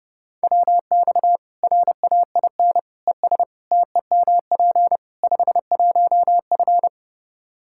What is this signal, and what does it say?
Morse code, heard as a keyed tone, about 30 wpm, 700 Hz. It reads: WX RAIN ES TEMP 51F
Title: Practice QSO - Exchange 2 - Info - 30wpm